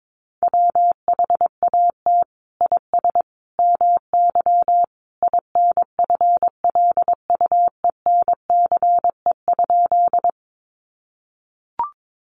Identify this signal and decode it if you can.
Morse code, heard as a keyed tone, about 22 wpm, 700 Hz. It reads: WHAT IS MY INFLUENCE?
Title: Sentences from Top 1100 Words 22wpm